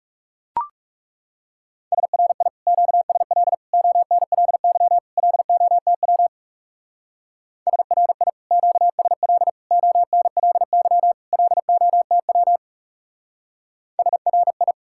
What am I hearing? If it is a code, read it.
Morse code, heard as a keyed tone, about 35 wpm, 700 Hz. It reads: SRI QSL ONLY LOTW SRI QSL ONLY LOTW SRI
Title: Speed-Racing Common QSO Phrases - 30wpm